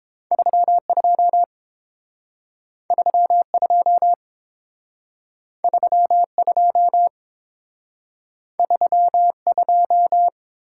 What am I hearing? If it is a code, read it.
Morse code, heard as a keyed tone, about 30 wpm, 700 Hz. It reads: 32 32 32 32